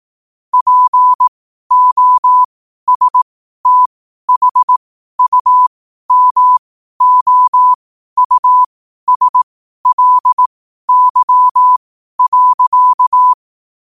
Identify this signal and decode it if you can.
Morse code, heard as a keyed tone, about 18 wpm, spaced as for 12 wpm, 1000 Hz. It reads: POSTHUMOUSLY.
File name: qotdfiles / 12wpm